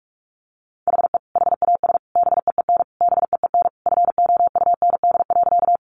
Morse code, heard as a keyed tone, about 44 wpm, 700 Hz. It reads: HE HAS BEEN BEEN FOUND.
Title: Sentences from Top 300 Words 45wpm